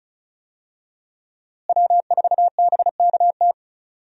Morse code, heard as a keyed tone, about 34 wpm, 700 Hz. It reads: W4BKT
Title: USA Call Signs - 5 Characters - Repeated 2x - 35wpm